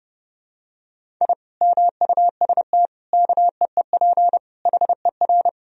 Morse code, heard as a keyed tone, about 30 wpm, 700 Hz. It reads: I MUST KEEP HER
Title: Sentences from Top 1000 Words - Review Part 1 - 30wpm